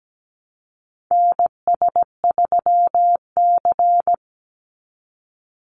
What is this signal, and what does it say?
Morse code, heard as a keyed tone, about 17 wpm, 700 Hz. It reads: NS3C
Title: Call Signs Repeated 2x - 17wpm